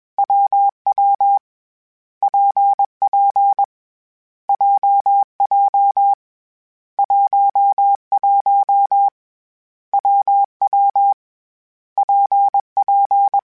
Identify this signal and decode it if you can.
Morse code, heard as a keyed tone, about 21 wpm, 800 Hz. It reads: WW PP JJ 11 WW PP
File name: Variable Speed / Variable Speed-mp3s / WPJ WPJ1 sp @20